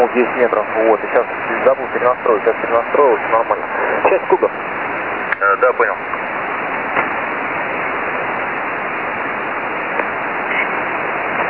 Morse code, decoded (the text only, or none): none